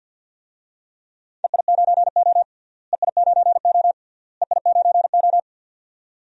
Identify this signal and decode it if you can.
Morse code, heard as a keyed tone, about 47 wpm, 700 Hz. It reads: EI9O EI9O EI9O
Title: DXCC-100 Call Signs - 4 Characters - Repeated 3x - 50wpm